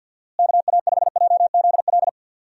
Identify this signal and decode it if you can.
Morse code, heard as a keyed tone, about 51 wpm, 700 Hz. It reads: KA5JZL
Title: USA Call Signs - 6 Characters - Repeated 2x - 50wpm